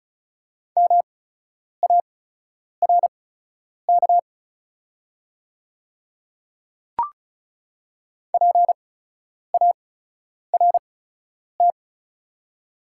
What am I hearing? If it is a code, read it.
Morse code, heard as a keyed tone, about 36 wpm, spaced as for 10 wpm, 700 Hz. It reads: MARK PART
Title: 8x Character Spacing - Top 300 Words - 35wpm